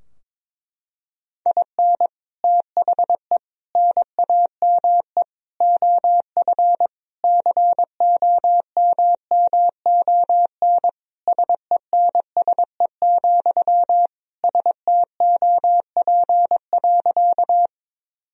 Morse code, heard as a keyed tone, about 22 wpm, 700 Hz. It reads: IN THE NAME OF COMMON SENSE, STOP.
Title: Sentences from Top 1000 Words - Review Part 4 - 22wpm